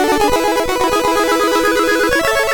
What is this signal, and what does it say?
There is no code.